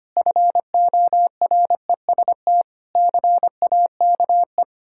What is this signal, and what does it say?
Morse code, heard as a keyed tone, about 25 wpm, 700 Hz. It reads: FOREST CAKE
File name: practice / 25wpm / Difficult Words with Hints - 25wpm